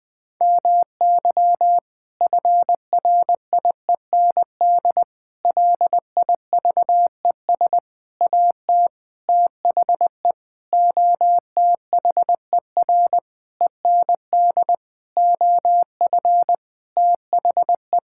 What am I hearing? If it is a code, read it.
Morse code, heard as a keyed tone, about 20 wpm, 700 Hz. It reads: MY FRIEND LIVES AT THE OTHER END OF THE
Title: Sentences with 5-Word Phrases - 20wpm